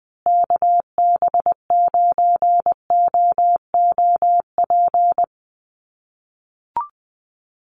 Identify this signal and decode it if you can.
Morse code, heard as a keyed tone, about 20 wpm, 700 Hz. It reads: KB9OOP
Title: USA Call Signs Repeated 3x 20wpm